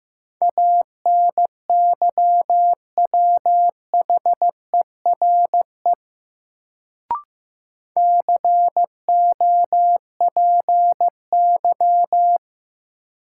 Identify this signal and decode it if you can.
Morse code, heard as a keyed tone, about 15 wpm, 700 Hz. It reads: ANYWHERE COPY
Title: Sentences from Top 1400 Words - New Word Drill - 15wpm